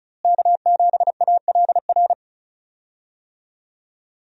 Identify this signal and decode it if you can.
Morse code, heard as a keyed tone, about 35 wpm, 700 Hz. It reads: K7ALR
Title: USA Call Signs - 5 Characters - Repeated 2x - 35wpm